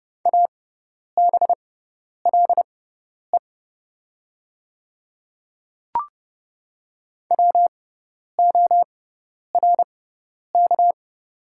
Morse code, heard as a keyed tone, about 30 wpm, spaced as for 10 wpm, 690 Hz. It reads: ABLE WORK
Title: 6x Character Spacing - Top 500 Words - 30wpm